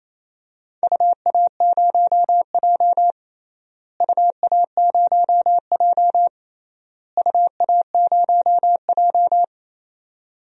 Morse code, heard as a keyed tone, about 28 wpm, 700 Hz. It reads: UA0J UA0J UA0J
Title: DXCC-100 Call Signs - 4 Characters - Repeated 3x - 28wpm